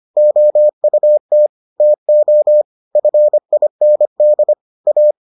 Morse code, heard as a keyed tone, about 25 wpm, 600 Hz. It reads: OUT TO FIND A